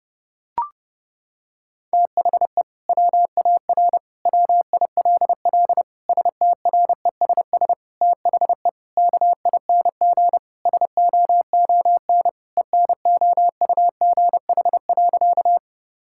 Morse code, heard as a keyed tone, about 30 wpm, 700 Hz. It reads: THE WAR WILL STRESS THE KING SOON ENOUGH.
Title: Sentences from Top 1000 Words - Review Part 6 - 30wpm